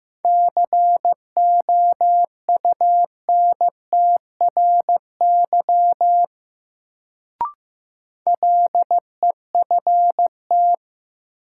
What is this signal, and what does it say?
Morse code, heard as a keyed tone, about 15 wpm, 700 Hz. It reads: COUNTRY LEFT
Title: Sentences from Top 300 Words - New Word Drill - 15wpm